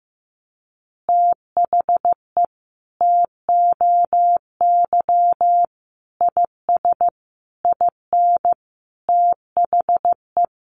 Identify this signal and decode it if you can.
Morse code, heard as a keyed tone, about 15 wpm, 700 Hz. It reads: THE TOY IS IN THE
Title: Easy Phrases 15wpm